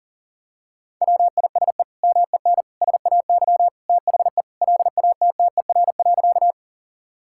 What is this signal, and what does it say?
Morse code, heard as a keyed tone, about 40 wpm, 700 Hz. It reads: WISE MEN SAY THE LATTER.
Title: Sentences from Top 1600 Words 40wpm